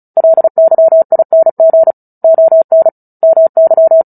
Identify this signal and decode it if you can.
Morse code, heard as a keyed tone, about 34 wpm, 650 Hz. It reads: LYING ON MY
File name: Book Two / 09-Chapter 9 - 35 WPM